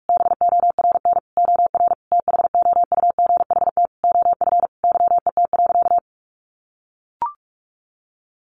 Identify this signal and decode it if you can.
Morse code, heard as a keyed tone, about 44 wpm, 700 Hz. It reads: BORN OR THOUGHT OF YET.
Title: A Sound Of Thunder 45wpm